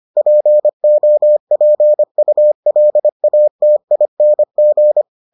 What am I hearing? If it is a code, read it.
Morse code, heard as a keyed tone, about 25 wpm, 600 Hz. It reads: POPULATING